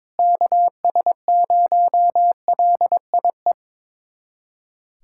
Morse code, heard as a keyed tone, about 22 wpm, 700 Hz. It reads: KS0LIE
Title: USA Callsign DE Callsign - 6 Characters - 22wpm